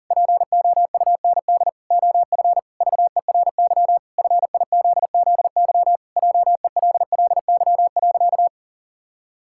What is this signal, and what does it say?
Morse code, heard as a keyed tone, about 40 wpm, 700 Hz. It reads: POUND OF VERY FIZZY JELLY.
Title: Pangram Challenge v2 - 40wpm